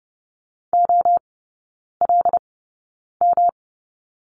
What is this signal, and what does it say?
Morse code, heard as a keyed tone, about 30 wpm, spaced as for 9 wpm, 700 Hz. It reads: OLM
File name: Lesson 18e - Letter m - Three Characters 30wpm